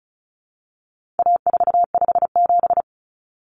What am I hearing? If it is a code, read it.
Morse code, heard as a keyed tone, about 34 wpm, 700 Hz. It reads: A457
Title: Common Cut Numbers - 4-Digit 35wpm